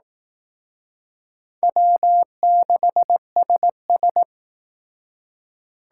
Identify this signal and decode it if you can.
Morse code, heard as a keyed tone, about 18 wpm, 700 Hz. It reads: W6SS
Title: Speed Racing - USA 1x2 Call Signs 15wpm